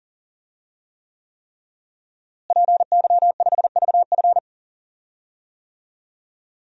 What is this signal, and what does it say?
Morse code, heard as a keyed tone, about 40 wpm, 700 Hz. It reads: PY5VF
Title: DXCC-100 Call Signs - 5 Characters - Repeated 3x - 40wpm